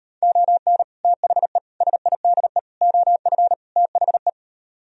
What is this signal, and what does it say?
Morse code, heard as a keyed tone, about 39 wpm, 700 Hz. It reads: ON THE SIDE OF THE